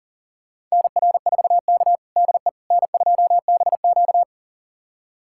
Morse code, heard as a keyed tone, about 40 wpm, 700 Hz. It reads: NR4K DE N2BQ
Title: USA Callsign DE Callsign - 4 Characters - 40wpm